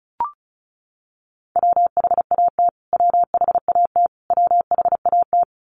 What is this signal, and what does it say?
Morse code, heard as a keyed tone, about 34 wpm, 700 Hz. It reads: WHAT WHAT WHAT
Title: Top 200 Words Repeated 35wpm